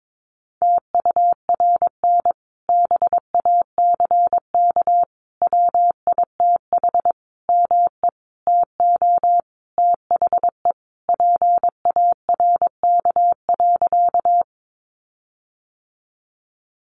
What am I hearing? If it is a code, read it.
Morse code, heard as a keyed tone, about 22 wpm, 700 Hz. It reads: TURN BACK WITH ME TO THE PARK.
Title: Sentences from Top 1300 Words 22wpm